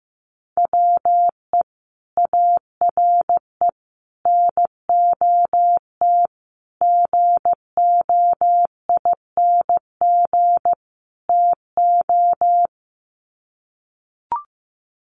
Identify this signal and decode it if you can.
Morse code, heard as a keyed tone, about 15 wpm, 700 Hz. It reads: WE ARE NOT GOING TO
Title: Sets of 5 Words 15wpm